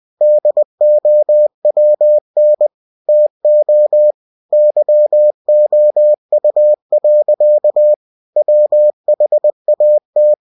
Morse code, heard as a keyed tone, about 20 wpm, 600 Hz. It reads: DOWN TO YOU. WHAT
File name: cw-books / winnie-the-pooh / winnie-20wpm-page-0006